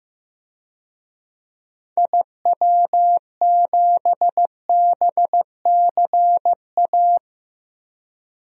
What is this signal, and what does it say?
Morse code, heard as a keyed tone, about 15 wpm, 700 Hz. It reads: IW7BCA
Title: DXCC-100 Call Signs - 6 Characters - Repeated 2x - 15wpm